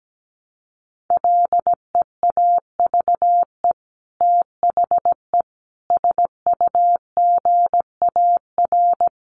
Morse code, heard as a keyed tone, about 17 wpm, 700 Hz. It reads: LEAVE THE SUGAR
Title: Sentences from Top 1200 Words 17wpm